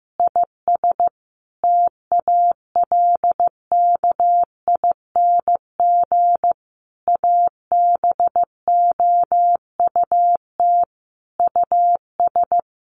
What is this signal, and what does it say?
Morse code, heard as a keyed tone, about 15 wpm, 700 Hz. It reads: IS TALKING ABOUT US
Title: Easy Phrases 15wpm